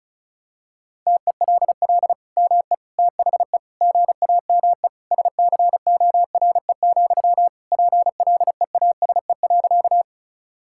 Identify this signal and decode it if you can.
Morse code, heard as a keyed tone, about 34 wpm, 700 Hz. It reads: TELL ME THE GAME SCORE, PLEASE.
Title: Sentences from Top 1600 Words 35wpm